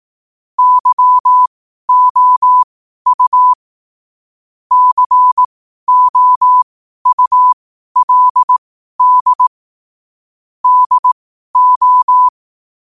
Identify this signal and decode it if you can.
Morse code, heard as a keyed tone, about 18 wpm, spaced as for 12 wpm, 1000 Hz. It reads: YOU COULD DO
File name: qotdfiles / 12wpm